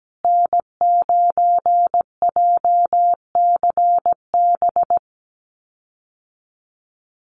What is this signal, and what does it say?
Morse code, heard as a keyed tone, about 17 wpm, 700 Hz. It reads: N9JCB
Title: USA Call Signs Repeated 2x 17wpm